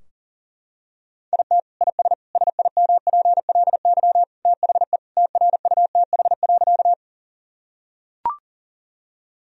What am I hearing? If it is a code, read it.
Morse code, heard as a keyed tone, about 37 wpm, 700 Hz. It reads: IT IS SIMPLY THE TRUTH.
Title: Sentences from Top 1000 Words - Review Part 4 - 40wpm